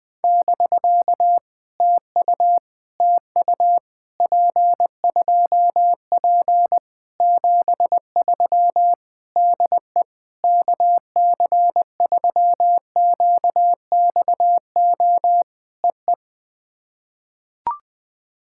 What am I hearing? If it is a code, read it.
Morse code, heard as a keyed tone, about 20 wpm, 700 Hz. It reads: <BK> TU TU P2P 73 DE KC3QXO EE
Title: POTA QSO Practice - 20wpm